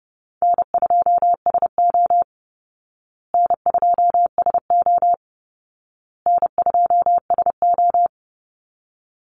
Morse code, heard as a keyed tone, about 30 wpm, 700 Hz. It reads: N2SO N2SO N2SO
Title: Easy Call Signs Repeated 3x - 30wpm